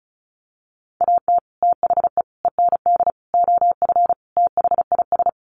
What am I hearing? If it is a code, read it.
Morse code, heard as a keyed tone, about 34 wpm, 700 Hz. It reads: AT THE END OF THIS
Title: Sets of 5 Words 35wpm